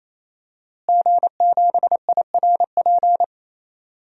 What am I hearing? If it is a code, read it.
Morse code, heard as a keyed tone, about 28 wpm, 700 Hz. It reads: G7IRP
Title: DXCC-100 Call Signs - 5 Characters - Repeated 3x - 28wpm